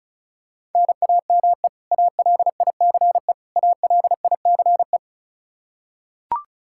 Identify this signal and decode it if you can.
Morse code, heard as a keyed tone, about 36 wpm, 700 Hz. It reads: NAME ALICE ALICE
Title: Practice QSO - Exchange 1 - Beginning - 35wpm